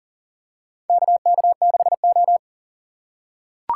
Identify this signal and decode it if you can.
Morse code, heard as a keyed tone, about 40 wpm, 700 Hz. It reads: KK6O
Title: USA Call Signs - 4 Characters - 3x Character Spacing - 40wpm